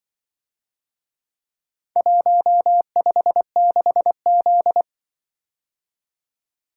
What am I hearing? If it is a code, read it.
Morse code, heard as a keyed tone, about 24 wpm, 700 Hz. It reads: 156Z
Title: Speed-Racing - Four Letter-Number Combinations - 20wpm